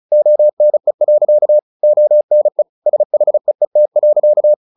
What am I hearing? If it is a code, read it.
Morse code, heard as a keyed tone, about 34 wpm, 600 Hz. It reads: ONE. ONE SHEET.
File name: the old man and the sea / the old man and the sea 1 35wpm